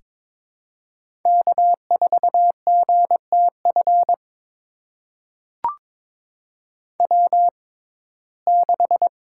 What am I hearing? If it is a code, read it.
Morse code, heard as a keyed tone, about 22 wpm, spaced as for 16 wpm, 700 Hz. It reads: K4GTF W6
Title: USA Call Signs - 5 Characters - 6x Character Spacing - 22wpm